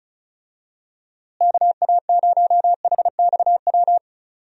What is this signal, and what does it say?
Morse code, heard as a keyed tone, about 34 wpm, 700 Hz. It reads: KA0HXW
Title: USA Call Signs - 6 Characters - Repeated 2x - 35wpm